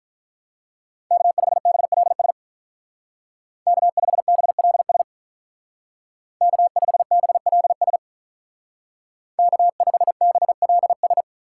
Morse code, heard as a keyed tone, about 47 wpm, 700 Hz. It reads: K5BLS K5BLS K5BLS K5BLS